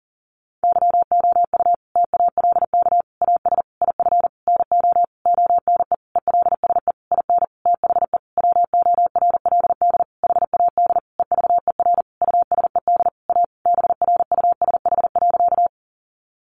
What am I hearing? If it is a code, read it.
Morse code, heard as a keyed tone, about 40 wpm, 700 Hz. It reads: YOU TALK AS IF NO ONE ELSE IN THE WORLD HAD EVER USED A BRUSH.
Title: Sentences from Top 1100 Words 40wpm